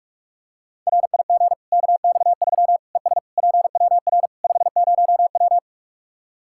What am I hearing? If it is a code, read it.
Morse code, heard as a keyed tone, about 46 wpm, 700 Hz. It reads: RIG KX3 ES PWR 50W
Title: Practice QSO - Exchange 2 - Info - 45wpm